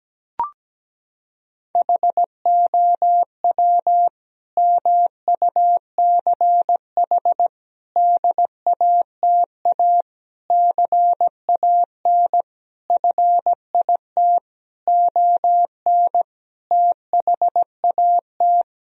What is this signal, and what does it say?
Morse code, heard as a keyed tone, about 17 wpm, 700 Hz. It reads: HOW MUCH DATA CAN FIT ON THAT
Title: Sentences from Top 1000 Words - Review Part 4 - 17wpm